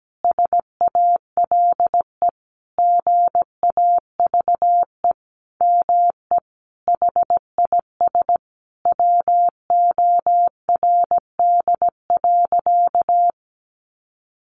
Morse code, heard as a keyed tone, about 17 wpm, 700 Hz. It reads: SALE GAVE ME HIS WORD.
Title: Sentences from Top 1400 Words 17wpm